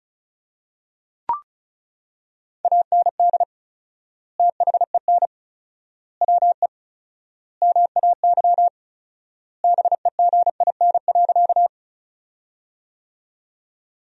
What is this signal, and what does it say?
Morse code, heard as a keyed tone, about 36 wpm, 700 Hz. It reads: AND THEN WE MAY BEGIN.
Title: 4x Word-Spacing - Sentences from Top 300 Words - 35wpm